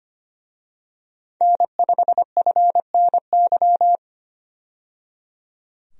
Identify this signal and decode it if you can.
Morse code, heard as a keyed tone, about 25 wpm, 700 Hz. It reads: N5FNY
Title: USA Call Signs - 5 Characters - 25wpm